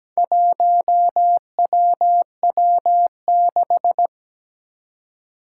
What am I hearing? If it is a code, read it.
Morse code, heard as a keyed tone, about 17 wpm, 700 Hz. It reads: 1WW6
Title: Four Challenging Letter-Number Combinations 17wpm